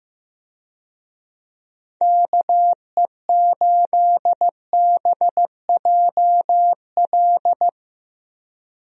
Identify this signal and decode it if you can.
Morse code, heard as a keyed tone, about 15 wpm, 700 Hz. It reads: KE8BJL
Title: USA Call Signs - 6 Characters - Repeated 2x - 15wpm